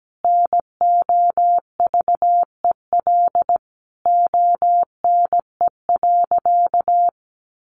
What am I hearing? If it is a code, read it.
Morse code, heard as a keyed tone, about 17 wpm, 700 Hz. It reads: NOVEL ONE.
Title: Sentences from Top 1300 Words 17wpm